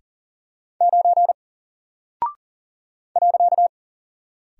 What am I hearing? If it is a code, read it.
Morse code, heard as a keyed tone, about 39 wpm, 700 Hz. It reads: 9 .
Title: Single Character (Letter:Number:Punctuation:Prosign) 40wpm